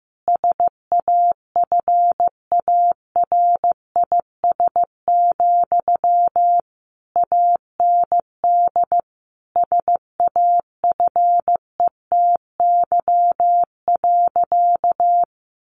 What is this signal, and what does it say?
Morse code, heard as a keyed tone, about 15 wpm, 700 Hz. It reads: SAFARIS, AND SAFETY.
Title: A Sound Of Thunder 15wpm